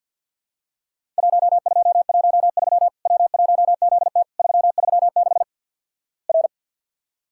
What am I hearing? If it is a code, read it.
Morse code, heard as a keyed tone, about 50 wpm, 700 Hz. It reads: 1213 W1ZT 336 R
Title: Worked All Europe DX Contest - QTC Exchange - 50wpm